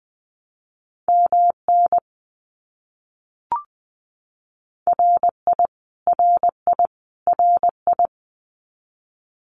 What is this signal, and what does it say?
Morse code, heard as a keyed tone, about 20 wpm, 700 Hz. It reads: MN RI RI RI
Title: US States Repeated 3x 20wpm